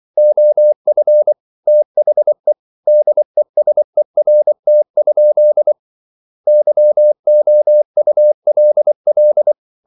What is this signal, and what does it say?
Morse code, heard as a keyed tone, about 24 wpm, 600 Hz. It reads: OF THE DESERT? YOULL